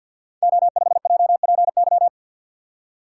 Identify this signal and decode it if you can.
Morse code, heard as a keyed tone, about 53 wpm, 700 Hz. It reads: O5JPY
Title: Code Groups - 5 Letter:Numbers - 50wpm